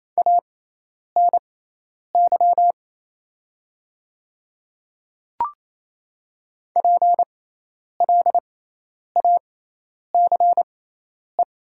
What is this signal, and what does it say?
Morse code, heard as a keyed tone, about 28 wpm, spaced as for 10 wpm, 700 Hz. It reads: ANY PLACE
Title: 6x Character Spacing - Top 300 Words - 28wpm